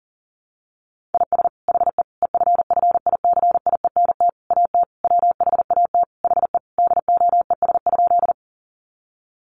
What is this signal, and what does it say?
Morse code, heard as a keyed tone, about 40 wpm, 700 Hz. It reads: IS HE EFFICIENT AT WHAT HE DOES?
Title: Sentences from Top 1300 Words 40wpm